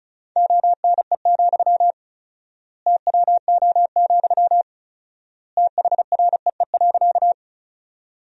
Morse code, heard as a keyed tone, about 34 wpm, 700 Hz. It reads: ONE, TWO, THREE.
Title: 4x Word-Spacing - Sentences from Top 200 Words - 35wpm